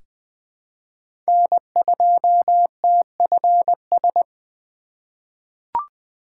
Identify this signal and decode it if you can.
Morse code, heard as a keyed tone, about 20 wpm, 700 Hz. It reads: N2TFS
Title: USA Call Signs - 5 Characters - 3x Character Spacing - 20wpm